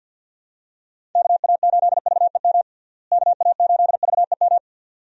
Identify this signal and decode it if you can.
Morse code, heard as a keyed tone, about 52 wpm, 700 Hz. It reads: KA8VEM KA8VEM
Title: USA Call Signs - 6 Characters - Repeated 2x - 50wpm